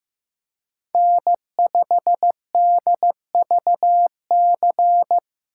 Morse code, heard as a keyed tone, about 15 wpm, 700 Hz. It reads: N5DVC
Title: USA Call Signs - 2x Character Spacing - 15wpm